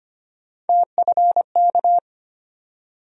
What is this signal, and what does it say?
Morse code, heard as a keyed tone, about 25 wpm, 700 Hz. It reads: TFK
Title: USA Call Sign Postfixes - 3 Letters 25wpm